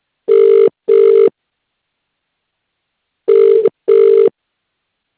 Ringback tone (UK), 400 + 450 Hz.